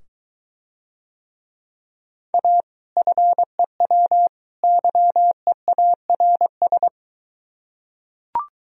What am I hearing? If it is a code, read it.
Morse code, heard as a keyed tone, about 23 wpm, 700 Hz. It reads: A FEW YEARS